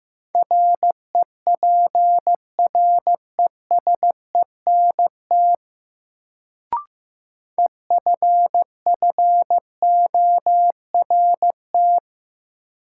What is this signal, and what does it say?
Morse code, heard as a keyed tone, about 15 wpm, 700 Hz. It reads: REPRESENT EFFORT
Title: Sentences from Top 1200 Words - New Word Drill Alt - 15wpm